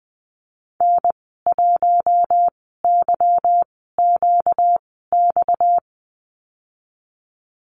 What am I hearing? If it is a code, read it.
Morse code, heard as a keyed tone, about 20 wpm, spaced as for 15 wpm, 700 Hz. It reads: N1YQX
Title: 238 Lesson 38g - Letter q - Call Signs 20wpm